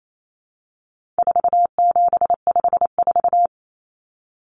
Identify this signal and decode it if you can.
Morse code, heard as a keyed tone, about 28 wpm, 700 Hz. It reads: 4754
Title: Common Cut Numbers - 4-Digit 28wpm